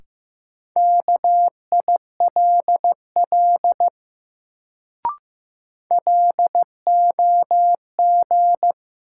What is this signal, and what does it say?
Morse code, heard as a keyed tone, about 15 wpm, 700 Hz. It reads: KILL LOG
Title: Sentences from Top 1400 Words - New Word Drill - 15wpm